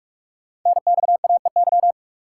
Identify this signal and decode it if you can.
Morse code, heard as a keyed tone, about 44 wpm, 700 Hz. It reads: NXAEY
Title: Code Groups - 5 Letters - 45wpm